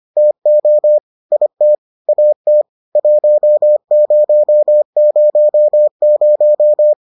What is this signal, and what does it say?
Morse code, heard as a keyed tone, about 25 wpm, 600 Hz. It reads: TO IT AT 1000